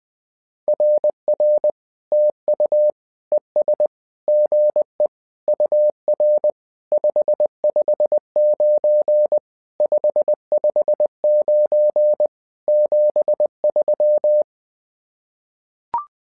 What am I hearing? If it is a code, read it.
Morse code, heard as a keyed tone, about 20 wpm, 600 Hz. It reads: RR TU ES GE UR 559 559 73
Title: SOTA QSO Practice 20wpm